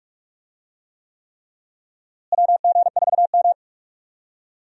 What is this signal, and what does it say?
Morse code, heard as a keyed tone, about 46 wpm, 700 Hz. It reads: WG4M
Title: USA Call Signs - 4 Characters - Repeated 3x - 45wpm